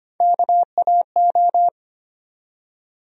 Morse code, heard as a keyed tone, about 25 wpm, 700 Hz. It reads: KAO
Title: USA Call Sign Postfixes - 3 Letters 25wpm